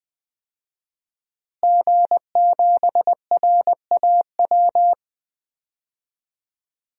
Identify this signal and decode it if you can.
Morse code, heard as a keyed tone, about 20 wpm, 700 Hz. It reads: G7RAW
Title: DXCC-100 Call Signs - 5 Characters - Repeated 3x - 20wpm